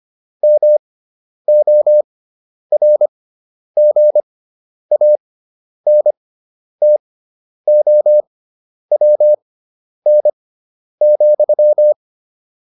Morse code, heard as a keyed tone, about 25 wpm, spaced as for 10 wpm, 600 Hz. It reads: MORGANTOWN,
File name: QSO302 10